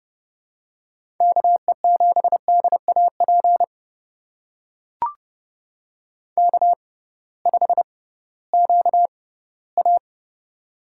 Morse code, heard as a keyed tone, about 30 wpm, spaced as for 17 wpm, 700 Hz. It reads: KE7DAP K5QA
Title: USA Call Signs - 6x Character Spacing - 30wpm